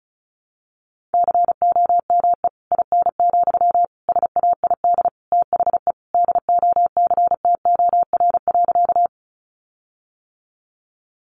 Morse code, heard as a keyed tone, about 34 wpm, 700 Hz. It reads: COME IN, SAID THE DOCTOR.
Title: Sentences from Top 1500 Words 35wpm